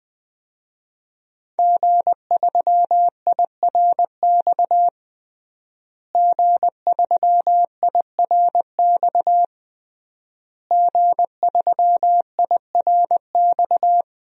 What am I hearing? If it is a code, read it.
Morse code, heard as a keyed tone, about 20 wpm, 700 Hz. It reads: G3IRX G3IRX G3IRX
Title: DXCC-100 Call Signs - 5 Characters - Repeated 3x - 20wpm